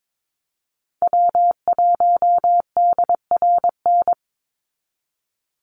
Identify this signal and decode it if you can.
Morse code, heard as a keyed tone, about 22 wpm, 700 Hz. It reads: W1DRN
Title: USA Call Signs 22wpm